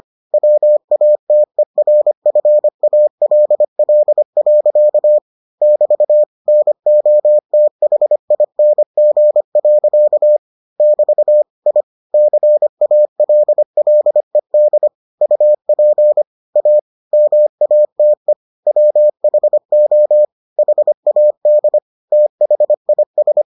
Morse code, heard as a keyed tone, about 25 wpm, 600 Hz. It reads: WATERFALL. <BT> NOTHING. <BT> I CALLED UP A MATE WHO HAD THIS